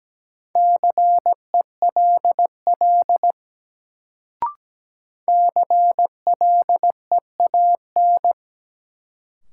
Morse code, heard as a keyed tone, about 17 wpm, 700 Hz. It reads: CELL CLEAN
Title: Sentences from Top 1100 Words - New Word Drill - 17wpm